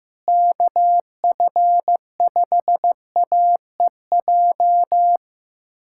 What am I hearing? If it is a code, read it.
Morse code, heard as a keyed tone, about 15 wpm, 700 Hz. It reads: KF5AEJ
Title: USA Callsign DE Callsign - 6 Characters - 15wpm